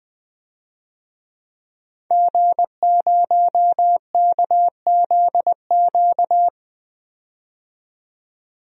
Morse code, heard as a keyed tone, about 21 wpm, 700 Hz. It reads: G0KZQ
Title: DXCC-100 Call Signs - 5 Characters - 20wpm